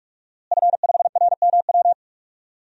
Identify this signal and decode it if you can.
Morse code, heard as a keyed tone, about 42 wpm, 700 Hz. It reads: F5RMW
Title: DXCC-100 Call Signs - 5 Characters - Repeated 3x - 45wpm